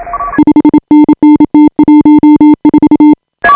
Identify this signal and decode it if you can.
Morse code, heard as a keyed tone, about 29 wpm, 310 Hz. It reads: 5NN T14
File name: AUDIO2018 / HA5BA CW